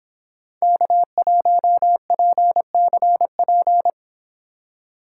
Morse code, heard as a keyed tone, about 29 wpm, 700 Hz. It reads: K1PCP